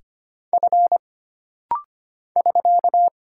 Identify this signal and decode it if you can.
Morse code, heard as a keyed tone, about 25 wpm, 700 Hz. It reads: F <SK>
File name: Single Character (Letter:Number:Punctuation:Prosign) - Rapid-Fire - 25wpm